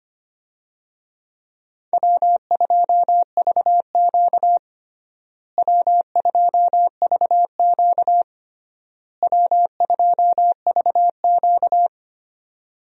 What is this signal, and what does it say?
Morse code, heard as a keyed tone, about 25 wpm, 700 Hz. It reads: W2VQ W2VQ W2VQ
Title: USA Call Signs - 4 Characters - Repeated 3x - 25wpm